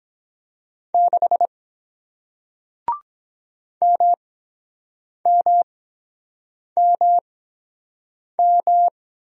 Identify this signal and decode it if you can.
Morse code, heard as a keyed tone, about 23 wpm, spaced as for 7 wpm, 700 Hz. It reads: 6 MMMM